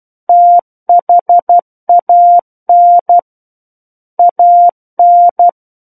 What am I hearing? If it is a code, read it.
Morse code, heard as a keyed tone, about 12 wpm, 700 Hz. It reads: THAN AN